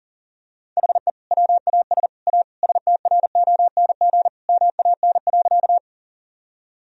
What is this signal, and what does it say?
Morse code, heard as a keyed tone, about 40 wpm, 700 Hz. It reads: HE WAS A STRONG MAN.
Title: Sentences from Top 1000 Words - Review Part 3 - 40wpm